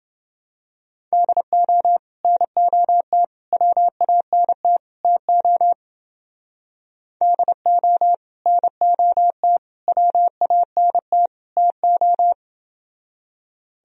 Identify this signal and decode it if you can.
Morse code, heard as a keyed tone, about 28 wpm, 700 Hz. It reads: DO NOT WANT TO DO NOT WANT TO